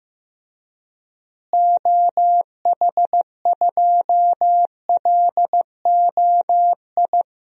Morse code, heard as a keyed tone, about 15 wpm, 700 Hz. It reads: OH2LOI
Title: DXCC-100 Call Signs - 6 Characters - Repeated 3x - 15wpm